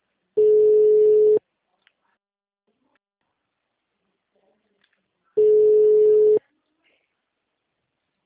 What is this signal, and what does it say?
Ringback tone (Europe, 425 Hz), with speech.